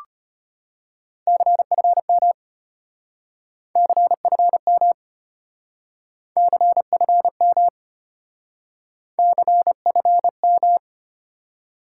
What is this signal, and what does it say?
Morse code, heard as a keyed tone, about 33 wpm, 700 Hz. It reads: CFM CFM CFM CFM